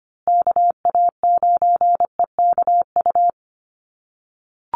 Morse code, heard as a keyed tone, about 25 wpm, 700 Hz. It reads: KA9EKU
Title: USA Call Signs Repeated 2x 25wpm